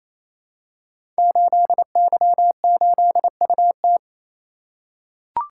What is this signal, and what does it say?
Morse code, heard as a keyed tone, about 27 wpm, 700 Hz. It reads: 8Y8UT
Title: Code Groups - 5 Letter:Numbers - 28wpm